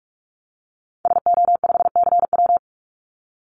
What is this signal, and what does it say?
Morse code, heard as a keyed tone, about 44 wpm, 700 Hz. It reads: SO5CW
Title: Call Signs - 45wpm